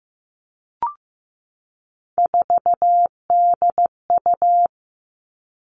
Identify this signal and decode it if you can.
Morse code, heard as a keyed tone, about 14 wpm, 700 Hz. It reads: 4DU
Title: Three Challenging Letter-Number Combinations 15wpm